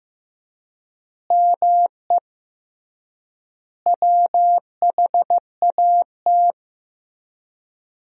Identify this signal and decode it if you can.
Morse code, heard as a keyed tone, about 16 wpm, 700 Hz. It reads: ME WHAT
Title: 3x Word-Spacing - Sentences from Top 200 Words - 15wpm